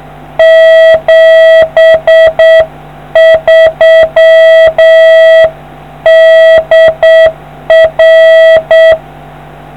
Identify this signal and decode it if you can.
Morse code, heard as a keyed tone, about 7 wpm, 650 Hz. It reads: 73 DR